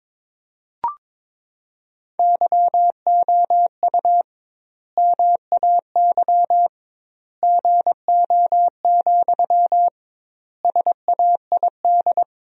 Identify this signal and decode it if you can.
Morse code, heard as a keyed tone, about 22 wpm, 700 Hz. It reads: YOU MAY GO, SAID
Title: 2x Word-Spacing - Sentences from Top 400 Words - 22wpm